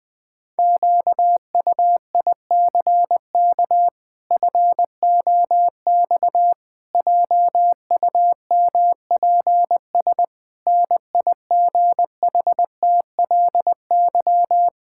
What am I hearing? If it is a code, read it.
Morse code, heard as a keyed tone, about 20 wpm, 700 Hz. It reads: QUICK FOX JUMPS NIGHTLY
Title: Pangram Challenge v2 - 20wpm